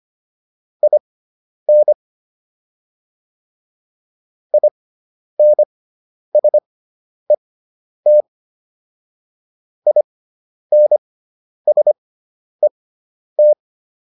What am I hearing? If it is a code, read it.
Morse code, heard as a keyed tone, about 25 wpm, spaced as for 10 wpm, 600 Hz. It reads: IN INSET INSET